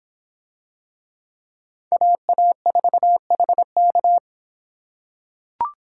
Morse code, heard as a keyed tone, about 24 wpm, 700 Hz. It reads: AA4HK